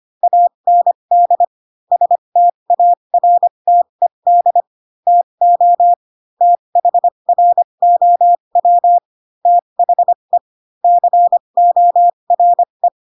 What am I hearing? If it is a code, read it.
Morse code, heard as a keyed tone, about 25 wpm, spaced as for 22 wpm, 700 Hz. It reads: AND STARTED TO THROW THE CORE
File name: Pinocchio / Pinocchio ch7 18 25 ik1hge